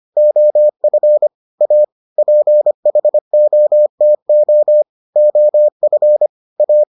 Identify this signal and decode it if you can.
Morse code, heard as a keyed tone, about 25 wpm, 600 Hz. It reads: OF A PHOTO OF A